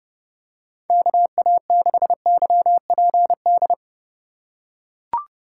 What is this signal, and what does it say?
Morse code, heard as a keyed tone, about 32 wpm, 700 Hz. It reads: KA6YPD
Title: USA Call Signs - 6 Characters - 3x Character Spacing - 30wpm